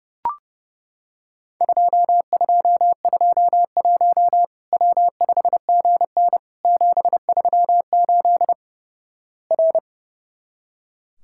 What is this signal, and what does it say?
Morse code, heard as a keyed tone, about 30 wpm, 700 Hz. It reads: 2221 W5GN 738 R
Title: Worked All Europe DX Contest - QTC Exchange - 30wpm